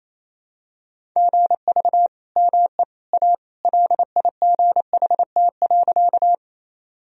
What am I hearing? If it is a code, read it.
Morse code, heard as a keyed tone, about 27 wpm, 700 Hz. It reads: GV ME A LIGHT.
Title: Sentences from Top 200 Words and Abbreviations 28wpm